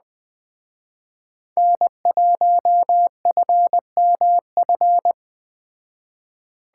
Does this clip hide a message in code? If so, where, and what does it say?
Yes: Morse code from 1.6 to 5.1 s, heard as a keyed tone, about 20 wpm, 700 Hz. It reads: N1FMF